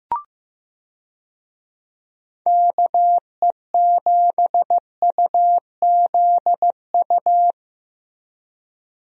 Morse code, heard as a keyed tone, about 15 wpm, 700 Hz. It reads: KE7UZU
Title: USA Call Signs - 6 Characters - Repeated 3x - 15wpm